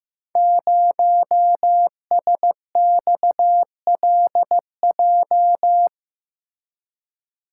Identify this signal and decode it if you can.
Morse code, heard as a keyed tone, about 15 wpm, 700 Hz. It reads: 0SXLJ
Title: Code Groups - 5 Letter:Numbers - 15wpm